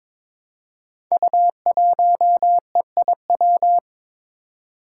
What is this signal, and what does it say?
Morse code, heard as a keyed tone, about 22 wpm, 700 Hz. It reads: U1EIW
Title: Five Challenging Letter-Number Combinations 22wpm